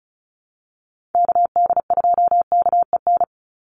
Morse code, heard as a keyed tone, about 34 wpm, 700 Hz. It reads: KD2KEN
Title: USA Call Signs 35wpm